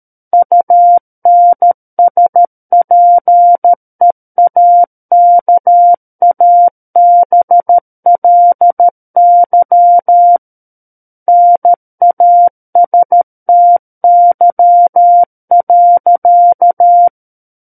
Morse code, heard as a keyed tone, about 13 wpm, 700 Hz. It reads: UNSPEAKABLY NASTY.
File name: Book One / 04-Chapter 4 - 13 WPM